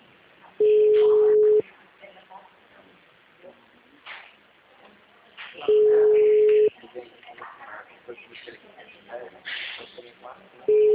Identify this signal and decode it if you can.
Ringback tone (Europe, 425 Hz), with speech.
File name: out- -2033- - - .30548